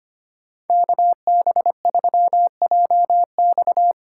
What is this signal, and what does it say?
Morse code, heard as a keyed tone, about 30 wpm, 700 Hz. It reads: KB3JX
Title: Speed Racing - USA 2x2 Call Signs 25wpm